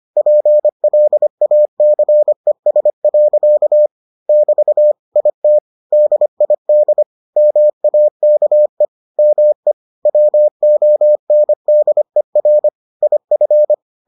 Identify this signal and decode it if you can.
Morse code, heard as a keyed tone, about 25 wpm, 600 Hz. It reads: PLACES. <BT> IT DID MAKE ME WONDER IF